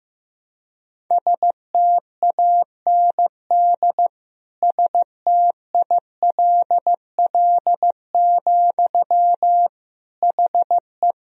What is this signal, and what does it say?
Morse code, heard as a keyed tone, about 15 wpm, 700 Hz. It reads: STAND STILL, HE
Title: Sentences from Top 1000 Words - Review Part 1 - 15wpm